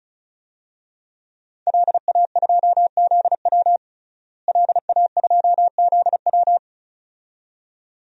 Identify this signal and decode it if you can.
Morse code, heard as a keyed tone, about 36 wpm, 700 Hz. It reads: LA2ZW LA2ZW
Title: DXCC-100 Call Signs - 5 Characters - Repeated 2x - 35wpm